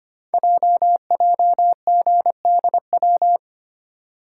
Morse code, heard as a keyed tone, about 25 wpm, 700 Hz. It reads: JJGDW
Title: Code Groups - 5 Letters - 25wpm